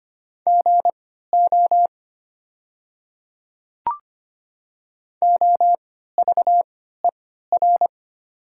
Morse code, heard as a keyed tone, about 25 wpm, spaced as for 14 wpm, 700 Hz. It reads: GO OVER
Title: 3x Character Spacing - Top 200 Words - 25wpm